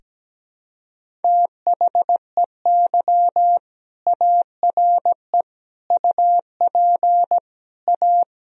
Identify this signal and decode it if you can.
Morse code, heard as a keyed tone, about 17 wpm, 700 Hz. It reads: THEY ARE UP A
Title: Sentences from Top 1000 Words - Review Part 1 - 17wpm